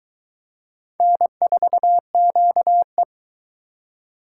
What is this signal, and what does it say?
Morse code, heard as a keyed tone, about 22 wpm, 700 Hz. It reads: N4QE